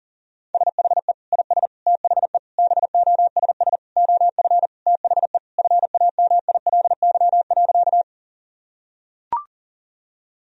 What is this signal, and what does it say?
Morse code, heard as a keyed tone, about 40 wpm, 700 Hz. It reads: SHE IS THE BOSS OF THE FAMILY.
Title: Sentences from Top 1000 Words - Review Part 4 - 40wpm